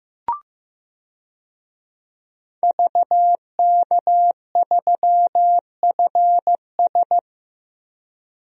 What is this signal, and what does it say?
Morse code, heard as a keyed tone, about 15 wpm, 700 Hz. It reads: VK3FS
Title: USA Call Signs - 5 Characters - Repeated 2x - 15wpm